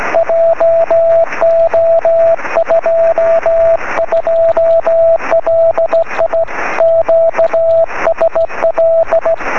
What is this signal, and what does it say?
Morse code, heard as a keyed tone, about 15 wpm, 670 Hz, with radio noise. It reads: JO22LI QSL
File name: sm7lcb / 20 23cm / 1544 pa3bfk 23cm